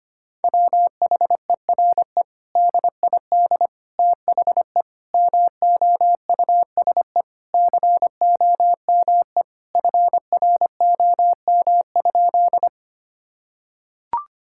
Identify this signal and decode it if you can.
Morse code, heard as a keyed tone, about 25 wpm, 700 Hz. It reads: WHERE DID THE MOUSE COME FROM?
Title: Sentences from Top 1000 Words - Review Part 5 - 25wpm